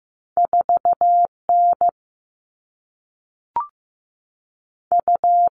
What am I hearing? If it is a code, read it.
Morse code, heard as a keyed tone, about 15 wpm, 700 Hz. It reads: 4 N U
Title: All Cut Numbers - 2-Digit 15wpm